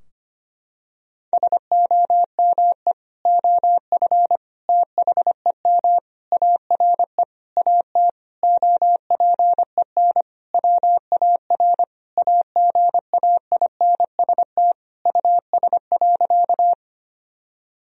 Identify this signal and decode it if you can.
Morse code, heard as a keyed tone, about 25 wpm, 700 Hz. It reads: SOME OF THEM ARE AT OPEN WAR AGAINST US.
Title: Sentences from Top 1000 Words - Review Part 2 - 25wpm